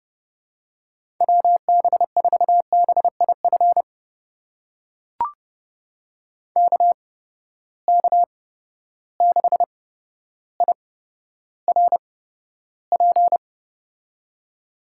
Morse code, heard as a keyed tone, about 32 wpm, spaced as for 13 wpm, 700 Hz. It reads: WB4BIF KK6IRP
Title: USA Call Signs - 6 Characters - 8x Character Spacing - 30wpm